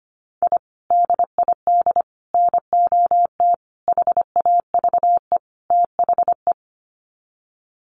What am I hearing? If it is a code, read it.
Morse code, heard as a keyed tone, about 25 wpm, 700 Hz. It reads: I DID NOT HAVE THE
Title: Sets of 5 Words Encore 25wpm